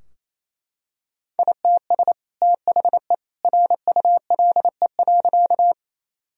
Morse code, heard as a keyed tone, about 28 wpm, 700 Hz. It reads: ITS THE RULE.
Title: Sentences from Top 1000 Words - Review Part 2 - 28wpm